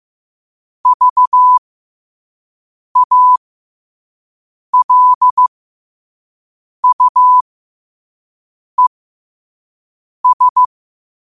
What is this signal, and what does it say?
Morse code, heard as a keyed tone, about 15 wpm, spaced as for 5 wpm, 1000 Hz. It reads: VALUES
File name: qotdfiles / 5wpm